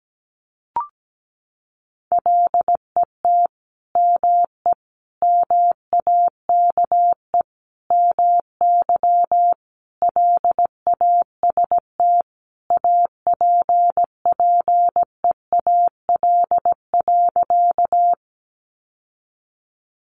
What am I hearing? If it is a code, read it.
Morse code, heard as a keyed tone, about 17 wpm, 700 Hz. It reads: LET ME MAKE MY LAST APPEAL.
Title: Sentences from Top 1300 Words 17wpm